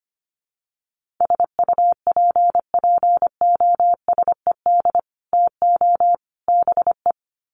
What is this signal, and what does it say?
Morse code, heard as a keyed tone, about 25 wpm, 700 Hz. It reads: SUPPOSED TO BE
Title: Sets of 3 Words 25wpm